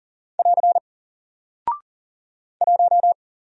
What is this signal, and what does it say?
Morse code, heard as a keyed tone, about 40 wpm, 700 Hz. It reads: <AR> 1
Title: Single Character (Letter:Number:Punctuation:Prosign) 40wpm